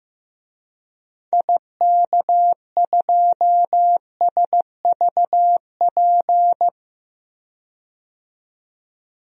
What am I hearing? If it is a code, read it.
Morse code, heard as a keyed tone, about 16 wpm, 700 Hz. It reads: IK2SVP
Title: DXCC-100 Call Signs - 6 Characters - 15wpm